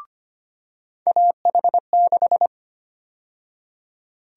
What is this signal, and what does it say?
Morse code, heard as a keyed tone, about 25 wpm, 700 Hz. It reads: AH6
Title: USA Call Sign Prefixes - 25wpm